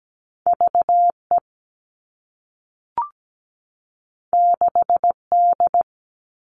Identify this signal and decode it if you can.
Morse code, heard as a keyed tone, about 17 wpm, 700 Hz. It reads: VE 6D
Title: All Cut Numbers - 2-Digit 17wpm